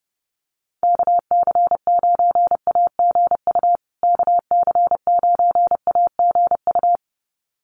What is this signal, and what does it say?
Morse code, heard as a keyed tone, about 30 wpm, 700 Hz. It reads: KC9AGU KC9AGU
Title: USA Call Signs Repeated 2x 30wpm